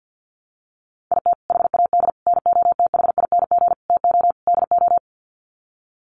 Morse code, heard as a keyed tone, about 50 wpm, 700 Hz. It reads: IT HAD NOTHING TO DO
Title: Sets of 5 Words Encore2 50wpm